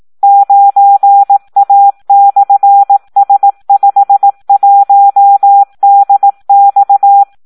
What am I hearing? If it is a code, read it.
Morse code, heard as a keyed tone, about 18 wpm, 800 Hz. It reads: 9A/S51DX
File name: Audio practica CW / SD 1 18